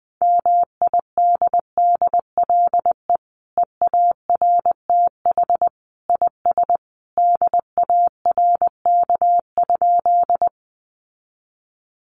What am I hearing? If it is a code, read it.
Morse code, heard as a keyed tone, about 20 wpm, 700 Hz. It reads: MIDDLE EARTH IS DARK?
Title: Sentences from Top 900 Words 20wpm